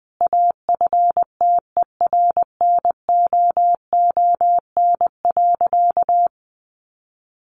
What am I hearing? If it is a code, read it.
Morse code, heard as a keyed tone, about 19 wpm, 700 Hz. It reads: AFTERNOON.
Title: Sentences from Top 1500 Words 20wpm